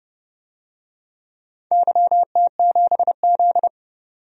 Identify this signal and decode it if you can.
Morse code, heard as a keyed tone, about 31 wpm, 700 Hz. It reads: YT7Z
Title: DXCC-100 Call Signs - 4 Characters - 30wpm